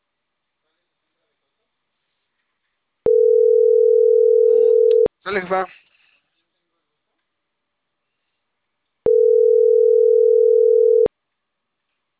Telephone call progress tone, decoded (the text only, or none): ringback tone